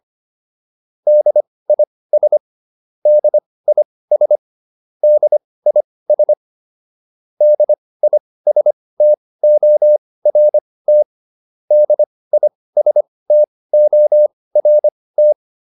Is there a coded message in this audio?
Yes: Morse code from 1.1 to 15.3 s, heard as a keyed tone, about 25 wpm, spaced as for 18 wpm, 600 Hz. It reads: DIS DIS DIS DISTORT DISTORT